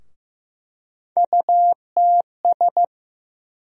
Morse code, heard as a keyed tone, about 15 wpm, 700 Hz. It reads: UTS
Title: USA Call Sign Postfixes - 3 Letters 15wpm